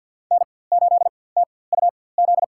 Morse code, heard as a keyed tone, about 52 wpm, spaced as for 23 wpm, 700 Hz. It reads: N8TUZ
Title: USA Call Signs - 4x Character Spacing - 50wpm